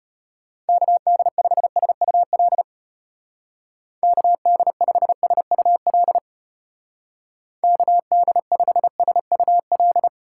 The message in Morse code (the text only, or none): KD5SUL KD5SUL KD5SUL